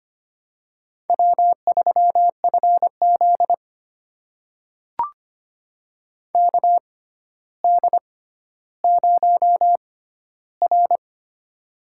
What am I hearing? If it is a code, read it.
Morse code, heard as a keyed tone, about 25 wpm, spaced as for 13 wpm, 700 Hz. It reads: W3FZ KD0R
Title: USA Call Signs - 4 Characters - 6x Character Spacing - 25wpm